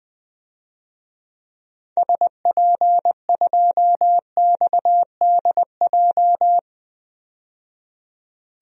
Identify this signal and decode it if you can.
Morse code, heard as a keyed tone, about 20 wpm, 700 Hz. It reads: SP2XDJ
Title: DXCC-100 Call Signs - 6 Characters - 20wpm